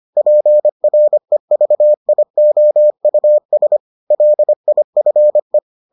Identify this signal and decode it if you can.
Morse code, heard as a keyed tone, about 25 wpm, 600 Hz. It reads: PREVIOUS LIFE